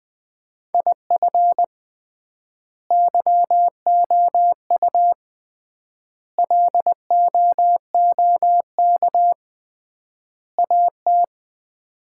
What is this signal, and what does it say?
Morse code, heard as a keyed tone, about 21 wpm, 700 Hz. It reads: IF YOU LOOK AT
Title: 3x Word-Spacing - Sets of 4 Words - 20wpm